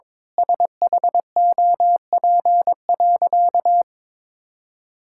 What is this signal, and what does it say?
Morse code, heard as a keyed tone, about 22 wpm, 700 Hz. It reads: SHOP.
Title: Sentences from Top 1600 Words 22wpm